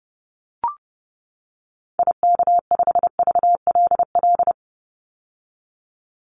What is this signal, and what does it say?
Morse code, heard as a keyed tone, about 30 wpm, 700 Hz. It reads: IK5VLL
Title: Call Signs - 30wpm